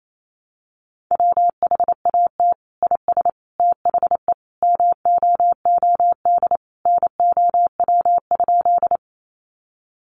Morse code, heard as a keyed tone, about 28 wpm, 700 Hz. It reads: WHAT IS THE MOOD NOW?
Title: Sentences from Top 1100 Words 28wpm